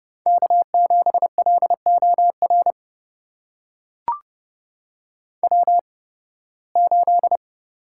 Morse code, heard as a keyed tone, about 30 wpm, spaced as for 20 wpm, 700 Hz. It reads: K7LOR W8
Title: USA Call Signs - 5 Characters - 8x Character Spacing - 30wpm